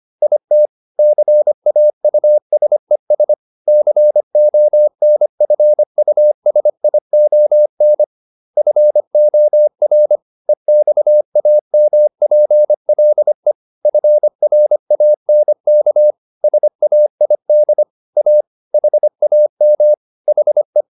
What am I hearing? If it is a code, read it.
Morse code, heard as a keyed tone, about 25 wpm, 600 Hz. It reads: IT CAUSES CONFUSION FOR EXAMPLE FRANK SAID A HAM HE